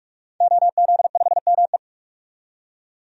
Morse code, heard as a keyed tone, about 44 wpm, 700 Hz. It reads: OZ5ME
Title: DXCC-100 Call Signs - 5 Characters - 45wpm